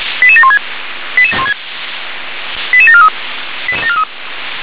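Five-tone selective call (ZVEI), heard as quick runs of five tones, tones of about 70 ms. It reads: 88716 88743
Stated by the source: Five-tone ZVEI selective call received on SDR.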